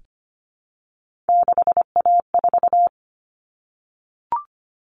Morse code, heard as a keyed tone, about 25 wpm, 700 Hz. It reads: 6A4
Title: Common Cut Numbers - 3-Digit 25wpm